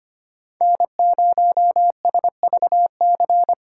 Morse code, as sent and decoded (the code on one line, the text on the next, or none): -. ----- ... ...- -.-.
N0SVC